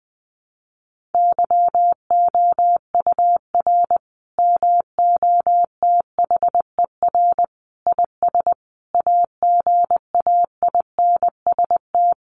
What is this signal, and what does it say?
Morse code, heard as a keyed tone, about 20 wpm, 700 Hz. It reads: YOUR MOTHER IS AGAINST
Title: Sentences from Top 1400 Words 20wpm